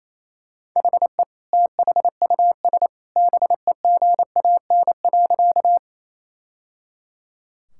Morse code, heard as a keyed tone, about 28 wpm, 700 Hz. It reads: HE THUS BEGAN.
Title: Sentences from Top 1000 Words - Review Part 5 - 28wpm